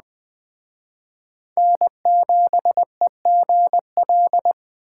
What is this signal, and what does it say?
Morse code, heard as a keyed tone, about 20 wpm, 700 Hz. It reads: N7EGL
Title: Speed Racing - USA 1x3 Call Signs 17wpm